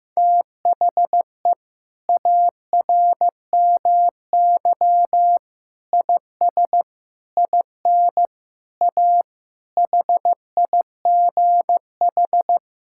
Morse code, heard as a keyed tone, about 15 wpm, 700 Hz. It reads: THE ARMY IS IN A HIGH
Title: Sentences from Top 1000 Words - Review Part 6 - 15wpm